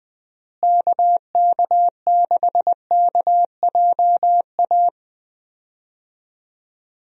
Morse code, heard as a keyed tone, about 21 wpm, 700 Hz. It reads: KK6KJA
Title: USA Call Signs - 6 Characters - Repeated 3x - 20wpm